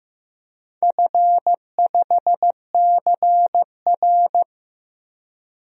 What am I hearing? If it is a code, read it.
Morse code, heard as a keyed tone, about 15 wpm, 700 Hz. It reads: F5CR
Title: DXCC-100 Call Signs - 4 Characters - 15wpm